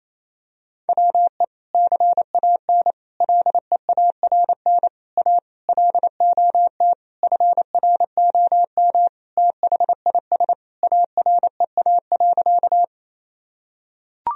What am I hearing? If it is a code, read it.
Morse code, heard as a keyed tone, about 28 wpm, 700 Hz. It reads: WE CAN LEARN A LOT FROM THIS AREA.
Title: Sentences from Top 1000 Words - Review Part 2 - 28wpm